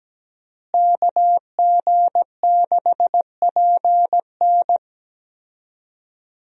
Morse code, heard as a keyed tone, about 17 wpm, 700 Hz. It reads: KG6PN
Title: USA Call Signs - 5 Characters - Repeated 3x - 17wpm